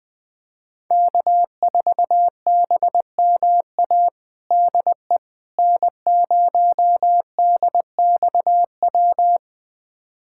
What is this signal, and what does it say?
Morse code, heard as a keyed tone, about 21 wpm, 700 Hz. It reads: K4BMA DE N0DXW
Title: USA Callsign DE Callsign - 5 Characters - 20wpm